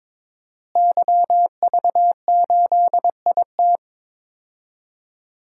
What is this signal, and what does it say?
Morse code, heard as a keyed tone, about 22 wpm, 700 Hz. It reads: YV8IT
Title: DXCC-100 Call Signs - 5 Characters - Repeated 3x - 22wpm